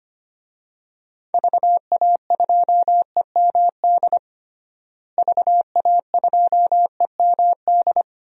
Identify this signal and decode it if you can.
Morse code, heard as a keyed tone, about 25 wpm, 700 Hz. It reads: VA2EMD VA2EMD
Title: DXCC-100 Call Signs - 6 Characters - Repeated 3x - 25wpm